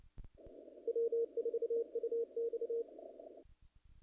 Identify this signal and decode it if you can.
Morse code, heard as a keyed tone, about 28 wpm, 460 Hz. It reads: W4UX
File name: contests / naqp-cw-k5tr-op-k5pi- -1800